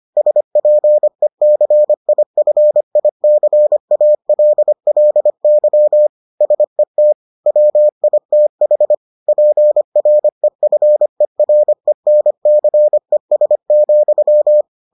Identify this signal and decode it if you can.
Morse code, heard as a keyed tone, about 25 wpm, 600 Hz. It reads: SPECIFICALLY SET WITH PREFERENCES,